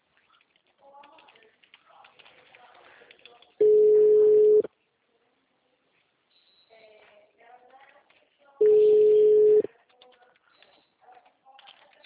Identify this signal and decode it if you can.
Ringback tone (Europe, 425 Hz), with speech.
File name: out- -2029- - - .5575